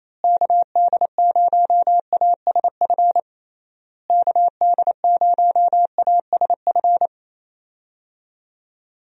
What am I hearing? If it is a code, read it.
Morse code, heard as a keyed tone, about 28 wpm, 700 Hz. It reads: KD0ASF KD0ASF
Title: USA Call Signs - 6 Characters - Repeated 3x - 28wpm